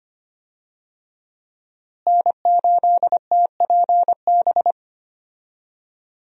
Morse code, heard as a keyed tone, about 25 wpm, 700 Hz. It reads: N8TPB
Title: USA Call Signs - 5 Characters - Repeated 3x - 25wpm